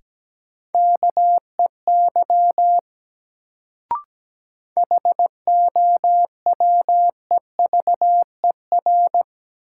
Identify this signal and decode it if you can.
Morse code, heard as a keyed tone, about 17 wpm, 700 Hz. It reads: KEY HOWEVER
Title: Sentences from Top 600 Words - New Word Drill - 17wpm